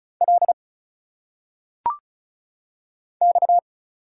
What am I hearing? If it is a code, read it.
Morse code, heard as a keyed tone, about 34 wpm, 700 Hz. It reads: L X
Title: Single Letters - Phonetic Pronunciation 35wpm